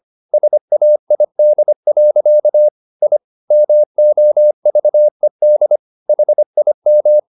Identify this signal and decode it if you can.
Morse code, heard as a keyed tone, about 25 wpm, 600 Hz. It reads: SAID. I MOVED HIM